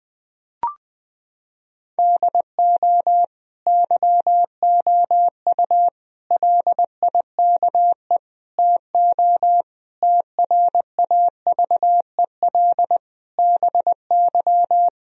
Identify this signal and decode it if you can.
Morse code, heard as a keyed tone, about 20 wpm, 700 Hz. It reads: DO YOU LIKE TO TRAVEL BY
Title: Sentences from Top 1000 Words - Review Part 3 - 20wpm